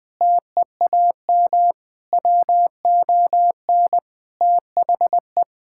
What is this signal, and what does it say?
Morse code, heard as a keyed tone, about 20 wpm, 700 Hz. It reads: TEAM WON THE
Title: Sentences with 8-Word Phrases - 20wpm